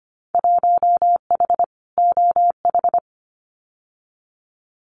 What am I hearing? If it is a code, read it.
Morse code, heard as a keyed tone, about 25 wpm, 700 Hz. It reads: 1H OH
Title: Winter Field Day Exchanges 25wpm